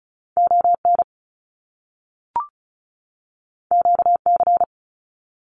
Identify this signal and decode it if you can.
Morse code, heard as a keyed tone, about 36 wpm, 700 Hz. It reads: ON QC
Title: Canadian Provinces and Territories 35wpm